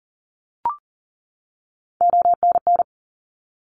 Morse code, heard as a keyed tone, about 40 wpm, 700 Hz. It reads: ONN
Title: ARRL Sections 40wpm